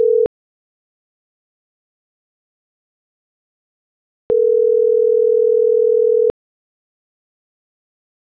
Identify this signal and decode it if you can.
Ringback tone (North America), 440 + 480 Hz.